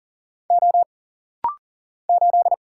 Morse code, heard as a keyed tone, about 40 wpm, 700 Hz. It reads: O 8
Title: Single Letter-Number Alt - Rapid-Fire 40wpm